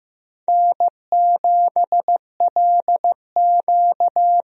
Morse code, heard as a keyed tone, about 15 wpm, 700 Hz. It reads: N7LQ
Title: USA Call Signs - 4 Characters - 15wpm